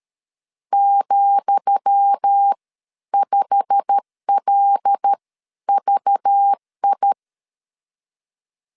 Morse code, heard as a keyed tone, about 13 wpm, spaced as for 10 wpm, 800 Hz. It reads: ,5LVI
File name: LEZIONI / 10 SIGNS / 10D COMA 12wpm 2544